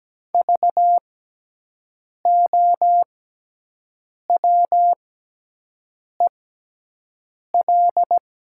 Morse code, heard as a keyed tone, about 17 wpm, spaced as for 6 wpm, 700 Hz. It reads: VOWEL
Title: 6x Character Spacing - Top 500 Words - 17wpm